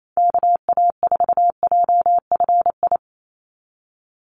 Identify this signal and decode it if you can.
Morse code, heard as a keyed tone, about 28 wpm, 700 Hz. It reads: KA4JFI
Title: USA Call Signs 28wpm